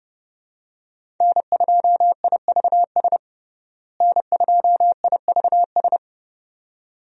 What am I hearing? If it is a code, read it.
Morse code, heard as a keyed tone, about 30 wpm, 700 Hz. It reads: N2IVS N2IVS
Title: USA Call Signs - 5 Characters - Repeated 2x - 30wpm